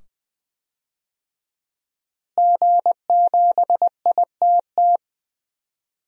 Morse code, heard as a keyed tone, about 20 wpm, 700 Hz. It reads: G7ITT
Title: DXCC-100 Call Signs - 5 Characters - Repeated 2x - 20wpm